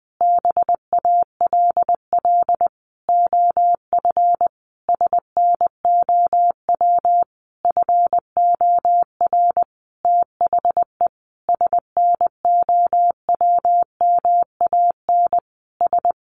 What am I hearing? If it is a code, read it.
Morse code, heard as a keyed tone, about 20 wpm, 700 Hz. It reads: BALL OF SNOW FOR THE SNOWMAN S
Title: Easy Phrases 20wpm